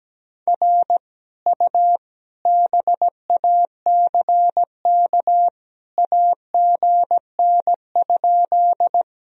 Morse code, heard as a keyed tone, about 17 wpm, 700 Hz. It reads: R U BACK AGN?
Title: Sentences from Top 200 Words and Abbreviations 17wpm